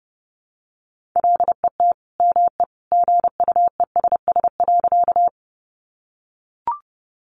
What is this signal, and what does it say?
Morse code, heard as a keyed tone, about 30 wpm, 700 Hz. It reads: LET ME GUESS.
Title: Sentences from Top 1500 Words 30wpm